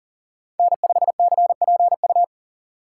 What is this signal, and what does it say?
Morse code, heard as a keyed tone, about 38 wpm, 700 Hz. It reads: N5CPU
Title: USA Call Signs - 5 Characters - 4x Character Spacing - 40wpm